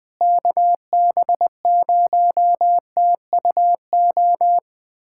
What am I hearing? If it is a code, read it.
Morse code, heard as a keyed tone, about 20 wpm, 700 Hz. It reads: KB0TUO
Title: USA Callsign DE Callsign 20wpm